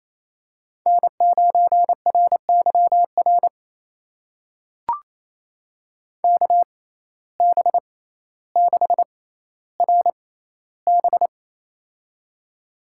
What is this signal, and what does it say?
Morse code, heard as a keyed tone, about 28 wpm, spaced as for 14 wpm, 700 Hz. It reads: N9RYR KB6RB
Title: USA Call Signs - 6x Character Spacing - 28wpm